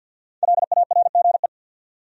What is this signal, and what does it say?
Morse code, heard as a keyed tone, about 61 wpm, 700 Hz. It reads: LARGE
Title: Speed-Racing Top 500 Words 50wpm